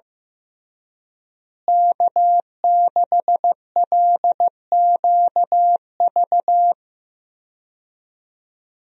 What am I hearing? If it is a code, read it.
Morse code, heard as a keyed tone, about 15 wpm, 700 Hz. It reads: K6LQV
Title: USA Call Signs - 5 Characters - Repeated 2x - 15wpm